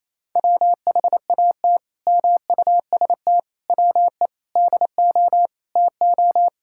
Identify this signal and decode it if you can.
Morse code, heard as a keyed tone, about 28 wpm, 700 Hz. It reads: WHAT MUST WE DO TO
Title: Sentences from Top 1000 Words - Review Part 6 - 28wpm